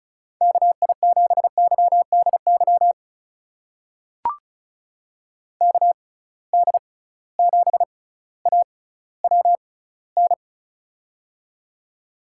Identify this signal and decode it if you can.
Morse code, heard as a keyed tone, about 36 wpm, spaced as for 18 wpm, 700 Hz. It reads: KI7YDY KD7AWN
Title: USA Call Signs - 6 Characters - 6x Character Spacing - 35wpm